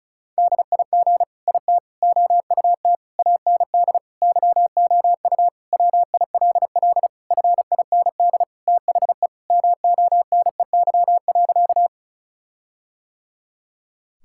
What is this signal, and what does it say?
Morse code, heard as a keyed tone, about 34 wpm, 700 Hz. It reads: DIG IT OUT AND YOU WILL FIND THE MONEY.
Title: Sentences from Top 1600 Words 35wpm